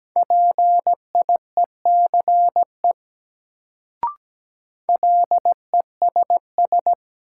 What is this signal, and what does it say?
Morse code, heard as a keyed tone, about 17 wpm, 700 Hz. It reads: PIECE LESS
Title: Sentences from Top 400 Words - New Word Drill - 17wpm